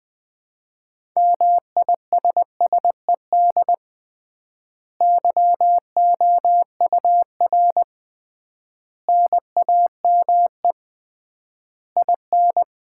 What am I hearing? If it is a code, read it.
Morse code, heard as a keyed tone, about 20 wpm, 700 Hz. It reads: MISSED YOUR NAME IN
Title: Common QSO Phrases - 3x Word-Spacing - 20wpm